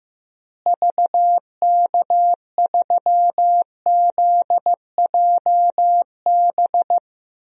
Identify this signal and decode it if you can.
Morse code, heard as a keyed tone, about 15 wpm, 700 Hz. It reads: VK3ZJB
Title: DXCC-100 Call Signs - 6 Characters - Repeated 2x - 15wpm